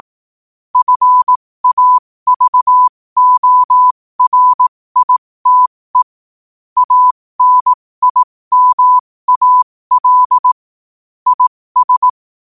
Morse code, heard as a keyed tone, about 18 wpm, spaced as for 15 wpm, 1000 Hz. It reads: FAVORITE ANIMAL IS
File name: qotdfiles / 15wpm